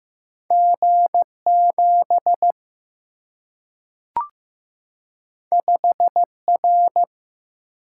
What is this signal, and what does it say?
Morse code, heard as a keyed tone, about 15 wpm, 700 Hz. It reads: G7 5R
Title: Single-Letter Single-Number 15wpm